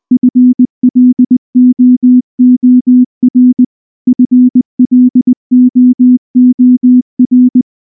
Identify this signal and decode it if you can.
Morse code, heard as a keyed tone, about 19 wpm, 250 Hz. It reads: FLOOR FLOOR
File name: morse / morse-training-20WPM-5